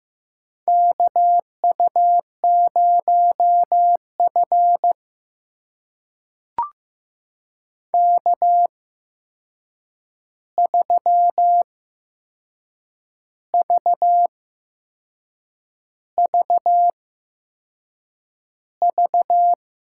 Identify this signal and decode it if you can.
Morse code, heard as a keyed tone, about 15 wpm, spaced as for 6 wpm, 700 Hz. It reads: KU0F K3VVV
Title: USA Call Signs - 8x Character Spacing - 15wpm